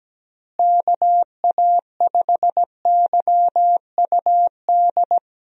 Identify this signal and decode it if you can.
Morse code, heard as a keyed tone, about 17 wpm, 700 Hz. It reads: KA5YUD
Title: USA Call Signs - 6 Characters - Repeated 2x - 17wpm